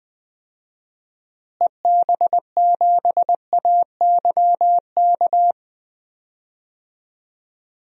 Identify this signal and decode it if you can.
Morse code, heard as a keyed tone, about 20 wpm, 700 Hz. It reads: EB7AYK
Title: DXCC-100 Call Signs - 6 Characters - Repeated 3x - 20wpm